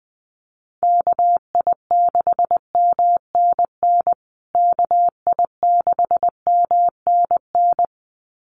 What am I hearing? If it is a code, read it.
Morse code, heard as a keyed tone, about 20 wpm, 700 Hz. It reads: KI6MNN KI6MNN
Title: USA Call Signs Repeated 2x 20wpm